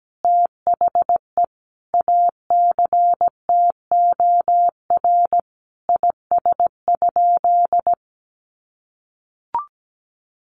Morse code, heard as a keyed tone, about 17 wpm, 700 Hz. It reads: THE ACTOR IS?
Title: Sentences from Top 1500 Words 17wpm